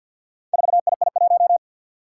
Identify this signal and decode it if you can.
Morse code, heard as a keyed tone, about 49 wpm, 700 Hz. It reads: 4II1
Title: Four Challenging Letter-Number Combinations 50wpm